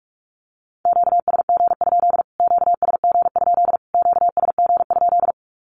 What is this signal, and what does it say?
Morse code, heard as a keyed tone, about 44 wpm, 700 Hz. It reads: QSG? QSG? QSG?
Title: Q-Codes Repeated 3x 45wpm